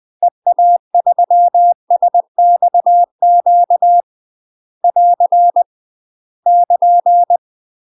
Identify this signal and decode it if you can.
Morse code, heard as a keyed tone, about 20 wpm, 700 Hz. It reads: EA3SXQ <AR> <KN>
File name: UK MASTER QSOs TRAINING / 20WPM / UK MASTER QSOs 0030